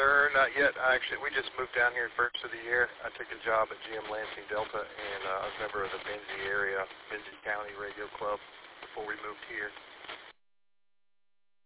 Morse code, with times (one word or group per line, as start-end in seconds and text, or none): none